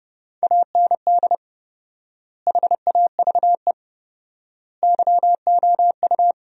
Morse code, heard as a keyed tone, about 30 wpm, 700 Hz. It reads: AND HAVE YOU
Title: 4x Word-Spacing - Sentences from Top 200 Words - 30wpm